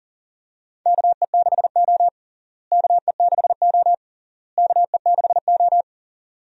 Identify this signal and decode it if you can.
Morse code, heard as a keyed tone, about 40 wpm, 700 Hz. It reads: KE6O KE6O KE6O
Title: USA Call Signs - 4 Characters - Repeated 3x - 40wpm